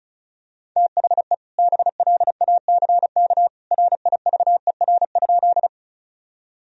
Morse code, heard as a keyed tone, about 36 wpm, 700 Hz. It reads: THE BLACK RIVER?
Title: Sentences from Top 1000 Words - Review Part 2 - 35wpm